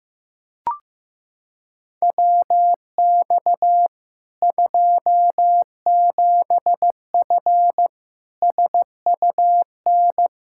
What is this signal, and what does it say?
Morse code, heard as a keyed tone, about 16 wpm, 700 Hz. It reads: WX 27F SUN
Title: Practice QSO - Exchange 2 - Info - 15wpm